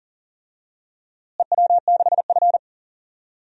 Morse code, heard as a keyed tone, about 38 wpm, 700 Hz. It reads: EW6F
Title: DXCC-100 Call Signs - 4 Characters - Repeated 2x - 40wpm